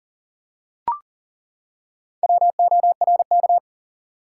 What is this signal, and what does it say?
Morse code, heard as a keyed tone, about 39 wpm, 700 Hz. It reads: WORK
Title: Four Letter Words - 40wpm